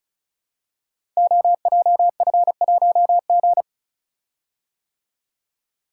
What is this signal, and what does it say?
Morse code, heard as a keyed tone, about 36 wpm, 700 Hz. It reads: OJF1G
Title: Code Groups - 5 Letter:Numbers - 35wpm